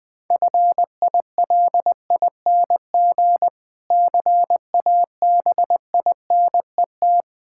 Morse code, heard as a keyed tone, about 20 wpm, 700 Hz. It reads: FILING CABINET
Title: Easy Words - Speed-Racing with Hints - 20wpm